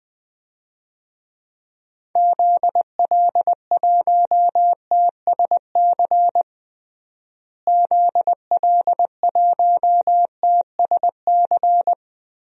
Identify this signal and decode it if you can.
Morse code, heard as a keyed tone, about 20 wpm, 700 Hz. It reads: ZL1TSC ZL1TSC
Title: DXCC-100 Call Signs - 6 Characters - Repeated 3x - 20wpm